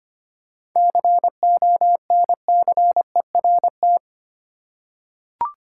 Morse code, heard as a keyed tone, about 25 wpm, 700 Hz. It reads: CONCERT
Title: Seven Letter Words - 25wpm